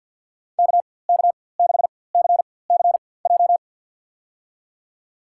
Morse code, heard as a keyed tone, about 52 wpm, spaced as for 23 wpm, 700 Hz. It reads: KK6CCJ
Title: USA Call Signs - 4x Character Spacing - 50wpm